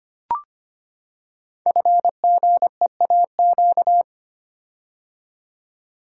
Morse code, heard as a keyed tone, about 25 wpm, 700 Hz. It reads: FGEAQ
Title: Code Groups - 5 Letters - 25wpm